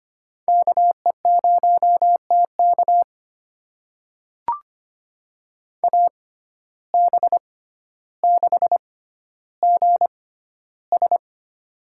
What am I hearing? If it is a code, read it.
Morse code, heard as a keyed tone, about 25 wpm, spaced as for 13 wpm, 700 Hz. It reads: KE0TK AB6GS
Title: USA Call Signs - 5 Characters - 6x Character Spacing - 25wpm